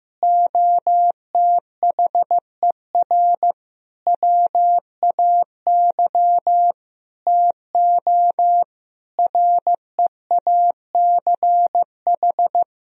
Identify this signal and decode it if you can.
Morse code, heard as a keyed tone, about 15 wpm, 700 Hz. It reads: OTHER WAY TO REACH
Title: Sentences from Top 1000 Words - Review Part 2 - 15wpm